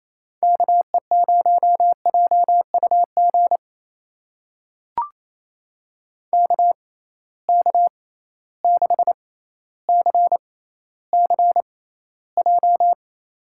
Morse code, heard as a keyed tone, about 28 wpm, spaced as for 14 wpm, 700 Hz. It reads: KE0JUG KK6CCJ
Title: USA Call Signs - 6 Characters - 6x Character Spacing - 28wpm